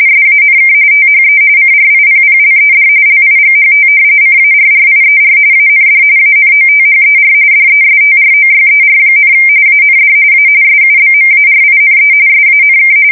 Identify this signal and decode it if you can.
RTTY, about 45.45 baud. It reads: SANK THE CRIPPLED LIGHT CARRIER AND ANOTHER DESTROYER, AND A SUBMARINE SANK
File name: fsk170 / battleshipnj